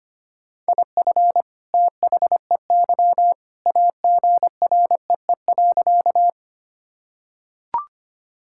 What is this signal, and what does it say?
Morse code, heard as a keyed tone, about 25 wpm, 700 Hz. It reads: IF THEY AGREE.
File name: practice / 25wpm / Sentences with 2-Word Phrases - 25wpm